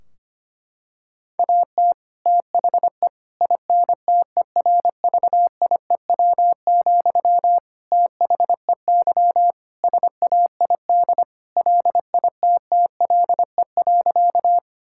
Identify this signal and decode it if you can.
Morse code, heard as a keyed tone, about 25 wpm, 700 Hz. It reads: AT THE INTERVIEW, THEY SAID LITTLE.
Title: Sentences from Top 1600 Words 25wpm